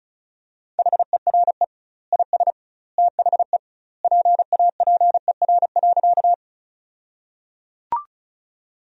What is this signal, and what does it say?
Morse code, heard as a keyed tone, about 36 wpm, 700 Hz. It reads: HERE IS THE PAPER.
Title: 2x Word-Spacing - Sentences from Top 300 Words - 35wpm